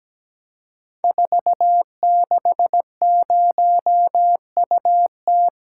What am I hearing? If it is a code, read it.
Morse code, heard as a keyed tone, about 17 wpm, 700 Hz. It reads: 460UT
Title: Code Groups - 5 Letter:Numbers - 17wpm